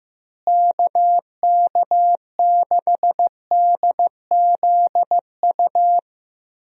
Morse code, heard as a keyed tone, about 15 wpm, 700 Hz. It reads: KK6DZU
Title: USA Call Signs - 6 Characters - Repeated 3x - 15wpm